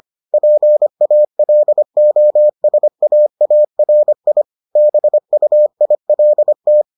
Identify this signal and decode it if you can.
Morse code, heard as a keyed tone, about 25 wpm, 600 Hz. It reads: PALOSAARI BUILT